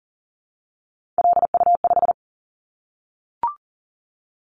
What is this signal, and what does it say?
Morse code, heard as a keyed tone, about 40 wpm, 700 Hz. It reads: LU5
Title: Callsign Prefixes for Top 100 DXCC - 40wpm